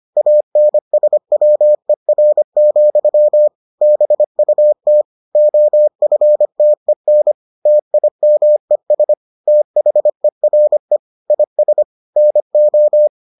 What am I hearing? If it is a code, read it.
Morse code, heard as a keyed tone, about 25 wpm, 600 Hz. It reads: ANSWER, BUT OFTEN TIMES THERE IS NO